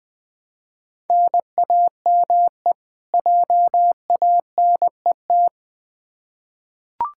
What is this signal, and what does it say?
Morse code, heard as a keyed tone, about 20 wpm, 700 Hz. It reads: NAME JANET
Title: K1USN SST QSO Practice 20wpm